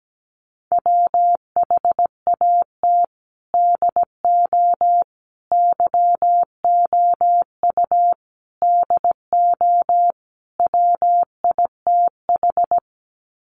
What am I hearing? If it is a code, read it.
Morse code, heard as a keyed tone, about 17 wpm, 700 Hz. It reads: WHAT DO YOU DO WITH
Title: Sets of 5 Words Encore2 17wpm